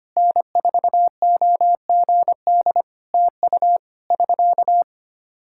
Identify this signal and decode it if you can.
Morse code, heard as a keyed tone, about 25 wpm, 700 Hz. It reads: N4OGD TU <SK>
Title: Practice QSO - Exchange 4 - Ending - 25wpm